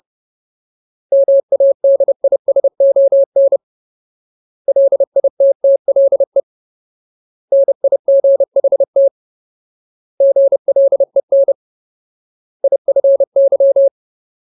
Morse code, heard as a keyed tone, about 30 wpm, 550 Hz. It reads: MADISON LITTLE NIGHT GLEN IFY
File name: Ultimate Word List / 30WPM Ultimate